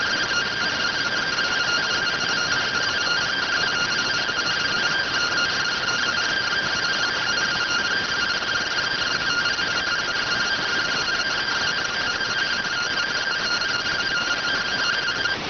Two-tone teleprinter signal, about 170 Hz shift.